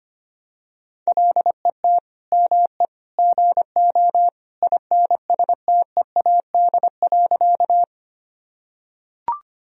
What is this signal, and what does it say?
Morse code, heard as a keyed tone, about 25 wpm, 700 Hz. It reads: LET ME GO INSTEAD.
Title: Sentences from Top 1000 Words - Review Part 4 - 25wpm